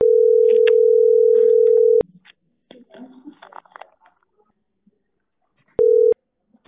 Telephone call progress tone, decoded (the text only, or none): ringback tone